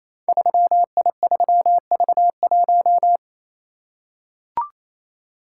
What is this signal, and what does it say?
Morse code, heard as a keyed tone, about 28 wpm, 700 Hz. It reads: 3I3V1
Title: Code Groups - 5 Letter:Numbers - 28wpm